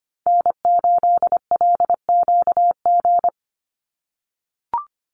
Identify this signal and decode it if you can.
Morse code, heard as a keyed tone, about 25 wpm, 700 Hz. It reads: N8LQG
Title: USA Call Signs 25wpm